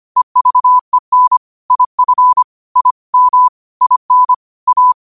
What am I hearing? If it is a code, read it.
Morse code, heard as a keyed tone, about 25 wpm, 1000 Hz. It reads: EVEN IF I M IN A